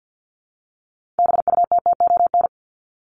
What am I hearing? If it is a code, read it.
Morse code, heard as a keyed tone, about 50 wpm, 700 Hz. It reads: BUTTON
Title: Top 1001-1250 Words 50wpm